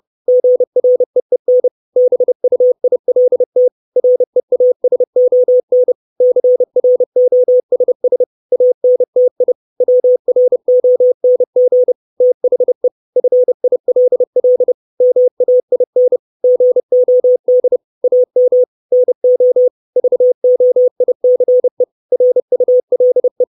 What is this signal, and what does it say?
Morse code, heard as a keyed tone, about 30 wpm, 500 Hz. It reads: GREEN BUILT REASON CROSS ANTI WRONG THE FILL MAIN GOD AM NO VOICE RULE
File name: 632 Words 30WPM